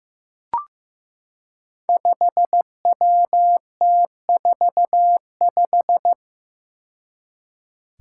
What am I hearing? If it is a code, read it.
Morse code, heard as a keyed tone, about 15 wpm, 700 Hz. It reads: 5WT45
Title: Five Challenging Letter-Number Combinations 15wpm